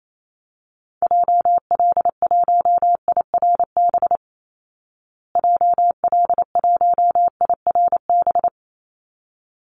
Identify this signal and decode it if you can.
Morse code, heard as a keyed tone, about 28 wpm, 700 Hz. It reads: JL1IRB JL1IRB
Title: Call Signs Repeated 3x - 28wpm